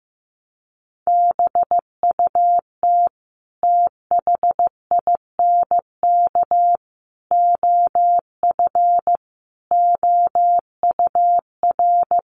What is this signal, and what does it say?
Morse code, heard as a keyed tone, about 15 wpm, 700 Hz. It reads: BUT THINK OF OUR
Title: Sentences from Top 1100 Words 15wpm